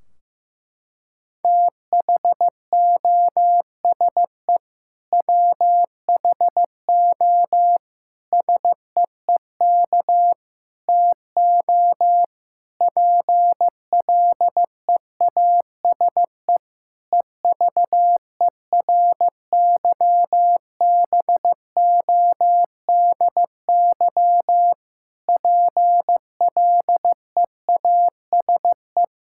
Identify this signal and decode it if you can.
Morse code, heard as a keyed tone, about 15 wpm, 700 Hz. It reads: THOSE WHO SEEK TO PLEASE EVERYBODY PLEASE
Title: Sentences from Top 1600 Words 15wpm